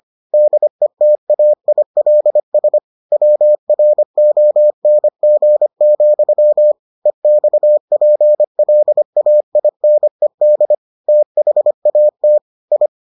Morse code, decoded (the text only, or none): DETAILS WRONG, EXPLAINED THAT I